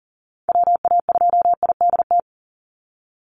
Morse code, heard as a keyed tone, about 40 wpm, 700 Hz. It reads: WA2IDT
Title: USA Call Signs 40wpm